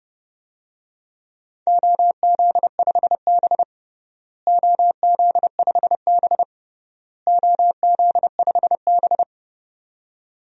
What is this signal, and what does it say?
Morse code, heard as a keyed tone, about 32 wpm, 700 Hz. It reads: OZ5B OZ5B OZ5B
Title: DXCC-100 Call Signs - 4 Characters - Repeated 3x - 30wpm